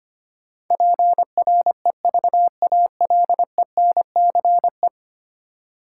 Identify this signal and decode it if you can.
Morse code, heard as a keyed tone, about 25 wpm, 700 Hz. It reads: PREVALENCE
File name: Ten Letter Words - 25wpm